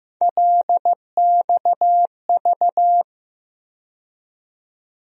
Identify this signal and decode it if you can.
Morse code, heard as a keyed tone, about 15 wpm, 700 Hz. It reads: LXV
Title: USA Call Sign Postfixes - 3 Letters 15wpm